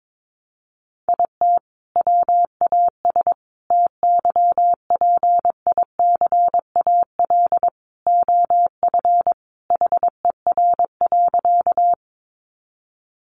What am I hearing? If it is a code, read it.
Morse code, heard as a keyed tone, about 22 wpm, 700 Hz. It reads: IT WAS TYPICAL OF HER.
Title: Sentences from Top 1100 Words 22wpm